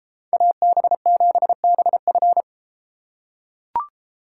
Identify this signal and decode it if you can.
Morse code, heard as a keyed tone, about 32 wpm, 700 Hz. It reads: AB7BF